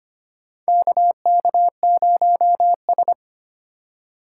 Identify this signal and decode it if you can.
Morse code, heard as a keyed tone, about 25 wpm, 700 Hz. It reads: KK0S
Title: USA Call Signs - 4 Characters - 6x Character Spacing - 25wpm